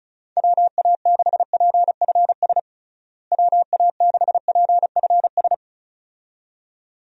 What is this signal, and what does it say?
Morse code, heard as a keyed tone, about 36 wpm, 700 Hz. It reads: WA6PFS WA6PFS
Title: USA Call Signs - 6 Characters - Repeated 2x - 35wpm